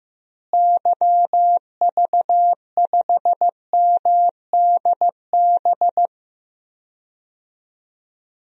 Morse code, heard as a keyed tone, about 15 wpm, 700 Hz. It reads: YV5MDB
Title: DXCC-100 Call Signs - 6 Characters - Repeated 2x - 15wpm